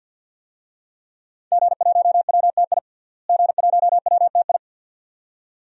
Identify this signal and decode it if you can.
Morse code, heard as a keyed tone, about 52 wpm, 700 Hz. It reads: G1WTI G1WTI
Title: DXCC-100 Call Signs - 5 Characters - Repeated 2x - 50wpm